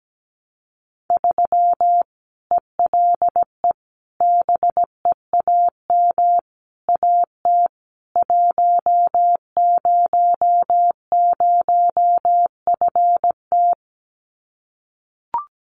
Morse code, heard as a keyed tone, about 17 wpm, 700 Hz. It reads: 3 ELE BEAM AT 100FT
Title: Common QSO Phrases 17wpm